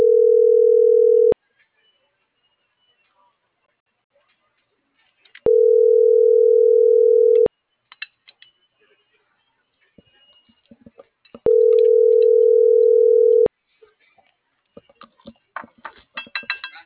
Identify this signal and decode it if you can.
Ringback tone (North America), 440 + 480 Hz.